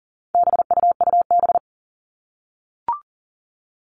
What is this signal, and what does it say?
Morse code, heard as a keyed tone, about 40 wpm, 700 Hz. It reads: BUUB
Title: All Cut Numbers - 4-Digit 40wpm